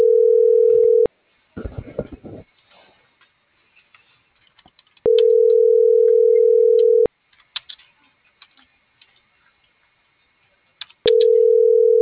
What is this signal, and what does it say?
Ringback tone (North America), 440 + 480 Hz.